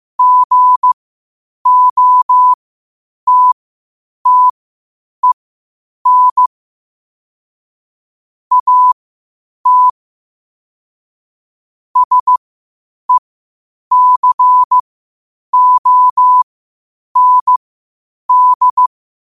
Morse code, heard as a keyed tone, about 15 wpm, spaced as for 8 wpm, 1000 Hz. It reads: GOTTEN AT SECOND
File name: qotdfiles / 8wpm